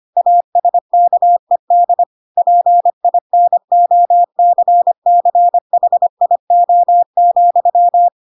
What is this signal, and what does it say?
Morse code, heard as a keyed tone, about 25 wpm, 700 Hz. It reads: ASKED PINOCCHIO,
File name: Pinocchio / Pinocchio ch4 25 25 ik1hge